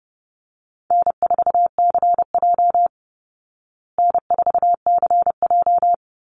Morse code, heard as a keyed tone, about 30 wpm, 700 Hz. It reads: N4CJ N4CJ
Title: Easy Call Signs Repeated 3x - 30wpm